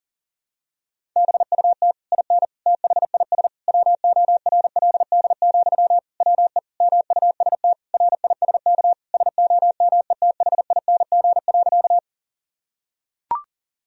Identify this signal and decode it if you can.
Morse code, heard as a keyed tone, about 37 wpm, 700 Hz. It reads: BUT IN THIS WORLD, WE MUST RISK SOMETHING.
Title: Sentences from Top 1000 Words - Review Part 4 - 40wpm